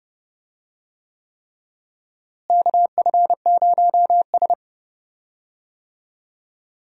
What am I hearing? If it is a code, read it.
Morse code, heard as a keyed tone, about 30 wpm, 700 Hz. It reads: KF0S
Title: USA Call Signs - 4 Characters - Repeated 3x - 30wpm